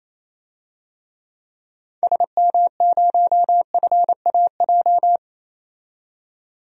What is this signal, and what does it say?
Morse code, heard as a keyed tone, about 28 wpm, 700 Hz. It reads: SM0FAJ
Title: DXCC-100 Call Signs - 6 Characters - 28wpm